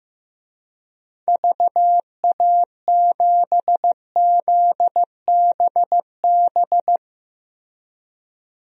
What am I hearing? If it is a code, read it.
Morse code, heard as a keyed tone, about 15 wpm, 700 Hz. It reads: VA7ZBB
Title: DXCC-100 Call Signs - 6 Characters - Repeated 2x - 15wpm